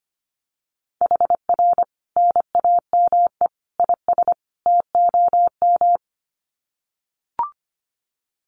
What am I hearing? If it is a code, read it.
Morse code, heard as a keyed tone, about 25 wpm, 700 Hz. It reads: HR NAME IS TOM
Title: Common QSO Phrases 25wpm